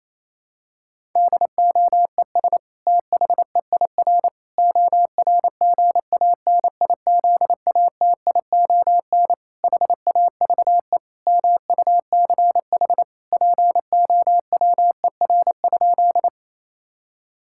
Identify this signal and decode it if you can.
Morse code, heard as a keyed tone, about 28 wpm, 700 Hz. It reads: DOES THEIR ORGANIZATION HAVE MUCH POWER?
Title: Sentences from Top 1000 Words - Review Part 5 - 28wpm